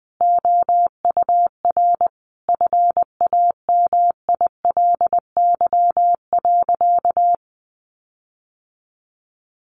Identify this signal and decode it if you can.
Morse code, heard as a keyed tone, about 20 wpm, 700 Hz. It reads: OUR FAMILY.
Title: Sentences from Top 1300 Words 20wpm